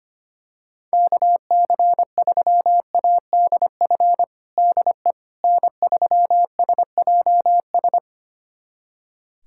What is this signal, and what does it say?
Morse code, heard as a keyed tone, about 25 wpm, 700 Hz. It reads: KC3ADF DE N3SJS
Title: USA Callsign DE Callsign 25wpm